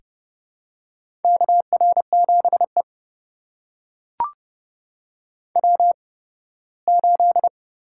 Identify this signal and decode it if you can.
Morse code, heard as a keyed tone, about 30 wpm, spaced as for 18 wpm, 700 Hz. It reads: KR7E W8
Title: USA Call Signs - 4 Characters - 8x Character Spacing - 30wpm